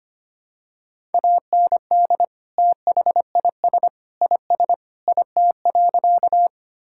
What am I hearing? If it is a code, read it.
Morse code, heard as a keyed tone, about 25 wpm, 700 Hz. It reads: AND THIS IS IT.
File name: practice / 25wpm / Sentences with 3-Word Phrases - 25wpm